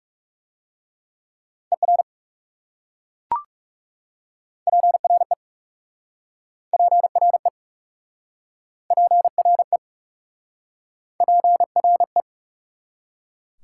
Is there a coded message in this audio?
Yes: Morse code from 1.7 to 12.2 s, heard as a keyed tone, about 40 wpm, 690 Hz. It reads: ER PRE PRE PRE PRE